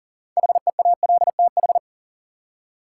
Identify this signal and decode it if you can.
Morse code, heard as a keyed tone, about 40 wpm, 700 Hz. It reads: HEALTH
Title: Six Letter Words - 40wpm